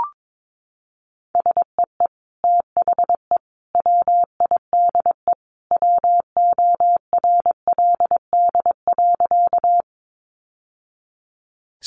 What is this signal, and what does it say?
Morse code, heard as a keyed tone, about 22 wpm, 700 Hz. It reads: SEE THE WIDE WORLD.
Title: Sentences from Top 900 Words 22wpm